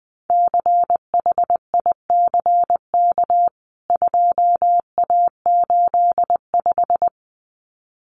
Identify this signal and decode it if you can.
Morse code, heard as a keyed tone, about 20 wpm, 700 Hz. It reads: CHICK 2A85
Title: CWT Exchange 20wpm